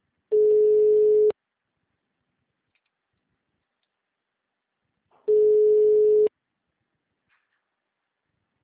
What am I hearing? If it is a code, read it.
Ringback tone (Europe, 425 Hz).
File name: out- -2006- - - .1373